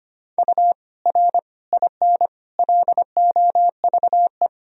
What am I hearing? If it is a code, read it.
Morse code, heard as a keyed tone, about 25 wpm, 700 Hz. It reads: U R IN LOVE
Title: Sentences from Top 400 Words and Abbreviations 25wpm